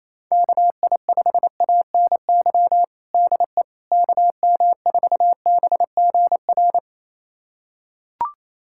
Morse code, heard as a keyed tone, about 28 wpm, 700 Hz. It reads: KI5ANY DE KM4BGR
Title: USA Callsign DE Callsign 28wpm